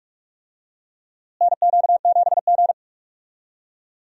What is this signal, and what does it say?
Morse code, heard as a keyed tone, about 45 wpm, 700 Hz. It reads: NQ7G
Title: USA Call Signs - 4 Characters - Repeated 3x - 45wpm